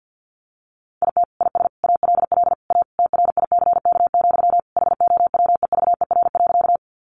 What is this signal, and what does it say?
Morse code, heard as a keyed tone, about 50 wpm, 700 Hz. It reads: IT IS ALL A TRICK, HOWEVER.
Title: Sentences from Top 1500 Words 50wpm